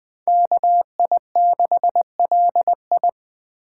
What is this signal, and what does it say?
Morse code, heard as a keyed tone, about 20 wpm, 700 Hz. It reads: KI6LI